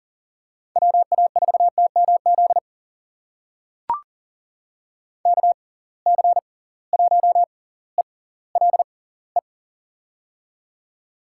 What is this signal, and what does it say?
Morse code, heard as a keyed tone, about 39 wpm, spaced as for 20 wpm, 700 Hz. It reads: WA4TMZ KC1ELE
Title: USA Call Signs - 6 Characters - 6x Character Spacing - 40wpm